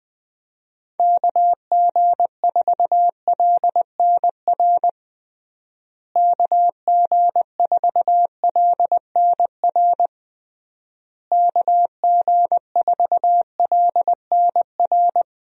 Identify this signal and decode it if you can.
Morse code, heard as a keyed tone, about 20 wpm, 700 Hz. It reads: KG4LNR KG4LNR KG4LNR
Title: USA Call Signs - 6 Characters - Repeated 3x - 20wpm